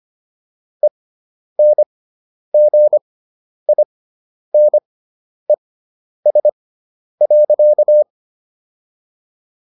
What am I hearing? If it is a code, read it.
Morse code, heard as a keyed tone, about 25 wpm, spaced as for 10 wpm, 600 Hz. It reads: ENGINES.